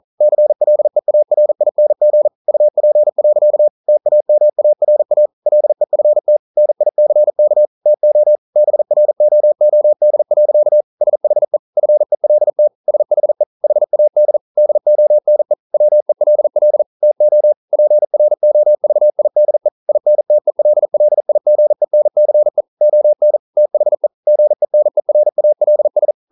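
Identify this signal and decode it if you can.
Morse code, heard as a keyed tone, about 42 wpm, 600 Hz. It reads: CLEARING UP. TAMARA LEFT NICK TO BROOD. SHE FELT SHE HAD DONE WELL TO PROVIDE INTELLIGENCE ON THE GENERALS
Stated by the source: Morse code practice at 41 wpm, from CW med Gnister.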